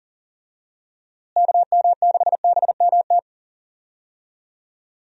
Morse code, heard as a keyed tone, about 40 wpm, 700 Hz. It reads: KM6BMT
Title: USA Call Signs - 6 Characters - Repeated 3x - 40wpm